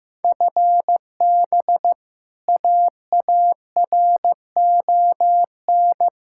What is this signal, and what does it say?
Morse code, heard as a keyed tone, about 15 wpm, 700 Hz. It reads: FB AARON
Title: Practice QSO - Exchange 3 - More Info - 15wpm